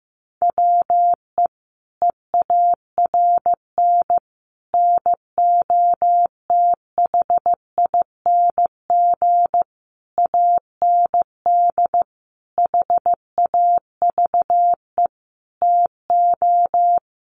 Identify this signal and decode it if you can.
Morse code, heard as a keyed tone, about 15 wpm, 700 Hz. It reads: WE EARN NOTHING AND HAVE TO
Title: Sentences from Top 1500 Words 15wpm